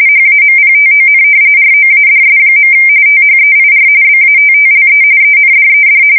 RTTY, about 45.45 baud. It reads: HAMPTON ROADS. THEIR BATTLE ENDED
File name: fsk170 / usnavy3